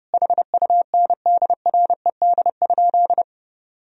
Morse code, heard as a keyed tone, about 32 wpm, 700 Hz. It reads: HUNDRED?
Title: Sentences from Top 1000 Words - Review Part 2 - 30wpm